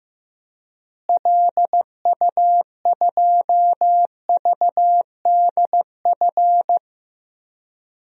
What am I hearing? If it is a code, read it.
Morse code, heard as a keyed tone, about 15 wpm, 700 Hz. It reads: LU2VDF
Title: DXCC-100 Call Signs - 6 Characters - Repeated 3x - 15wpm